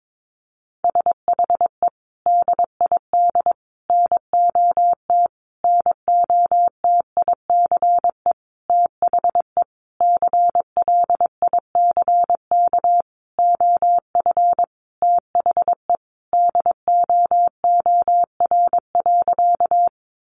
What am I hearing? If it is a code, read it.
Morse code, heard as a keyed tone, about 22 wpm, 700 Hz. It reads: SHE DID NOT NOTICE THE CLICK OF THE DOOR.
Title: Sentences from Top 1100 Words 22wpm